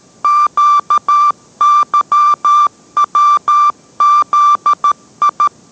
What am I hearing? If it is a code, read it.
Morse code, heard as a keyed tone, about 13 wpm, 1180 Hz. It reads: QYWZI